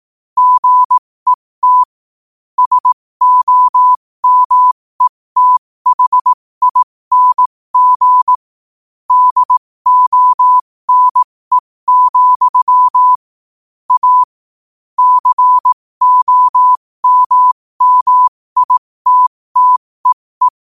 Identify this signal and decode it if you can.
Morse code, heard as a keyed tone, about 18 wpm, spaced as for 15 wpm, 1000 Hz. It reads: GET SOMETHING DONE, A COMMITTEE
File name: qotdfiles / 15wpm